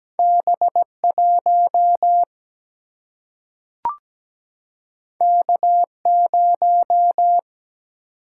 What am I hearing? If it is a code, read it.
Morse code, heard as a keyed tone, about 17 wpm, 700 Hz. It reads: B1 K0
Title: Single-Letter Single-Number 17wpm